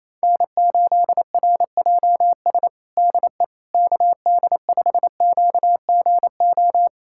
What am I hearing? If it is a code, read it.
Morse code, heard as a keyed tone, about 27 wpm, 700 Hz. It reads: N8RJS DE KD5QGO
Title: USA Callsign DE Callsign 28wpm